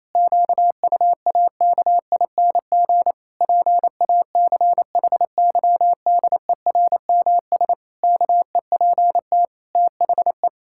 Morse code, heard as a keyed tone, about 28 wpm, 700 Hz. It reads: QUAKING PACHYDERMS KEPT THE
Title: Pangram Challenge v2 - 28wpm